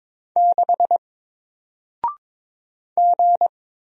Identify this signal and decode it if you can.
Morse code, heard as a keyed tone, about 22 wpm, 700 Hz. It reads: 6 G
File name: Single-Letter-Number-22wpm